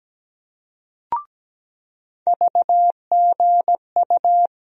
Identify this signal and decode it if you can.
Morse code, heard as a keyed tone, about 17 wpm, 700 Hz. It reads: VGU
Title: USA Call Sign Postfixes - 3 Letters 17wpm